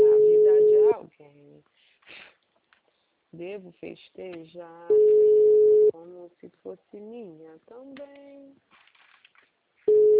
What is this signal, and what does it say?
Ringback tone (Europe, 425 Hz), with speech.